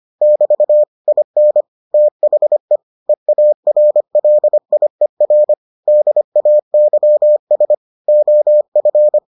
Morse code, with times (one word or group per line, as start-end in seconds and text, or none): 0.2-0.8 <BT>
1.1-1.6 IN
1.9-2.8 THE
3.1-5.5 EARLIER
5.9-7.7 DAYS
8.1-9.2 OF